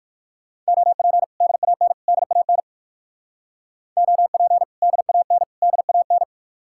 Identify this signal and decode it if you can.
Morse code, heard as a keyed tone, about 64 wpm, spaced as for 56 wpm, 700 Hz. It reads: OP DAN DAN OP DAN DAN
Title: Speed-Racing Common QSO Phrases Encore - 45wpm